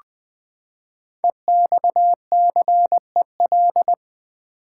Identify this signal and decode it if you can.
Morse code, heard as a keyed tone, about 20 wpm, 700 Hz. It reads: EXCEL
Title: Build A Word - 5 Letters At A Time - 10-Letter Words - 20wpm